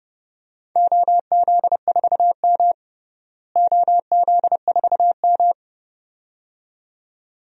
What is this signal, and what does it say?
Morse code, heard as a keyed tone, about 32 wpm, 700 Hz. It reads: OZ4M OZ4M
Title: DXCC-100 Call Signs - 4 Characters - Repeated 2x - 30wpm